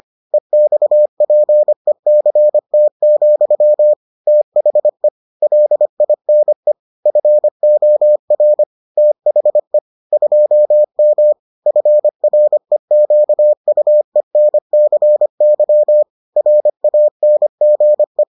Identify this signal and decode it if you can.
Morse code, heard as a keyed tone, about 25 wpm, 600 Hz. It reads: EXPECT, THE LINE FOR THE 2M FREQUENCY RANGE